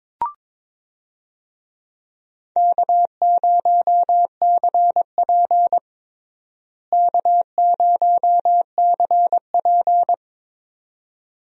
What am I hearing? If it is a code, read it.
Morse code, heard as a keyed tone, about 22 wpm, 700 Hz. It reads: K0CP K0CP
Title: USA Call Signs - 4 Characters - Repeated 2x - 22wpm